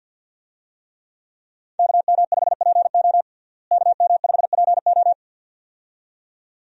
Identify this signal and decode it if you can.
Morse code, heard as a keyed tone, about 52 wpm, 700 Hz. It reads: KM5PO KM5PO
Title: USA Call Signs - 5 Characters - Repeated 2x - 50wpm